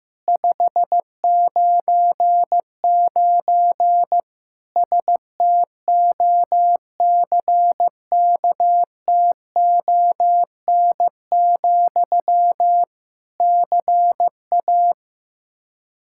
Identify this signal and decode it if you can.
Morse code, heard as a keyed tone, about 15 wpm, 700 Hz. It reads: 599 STOCKTON, CA
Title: K1USN SST QSO Practice 15wpm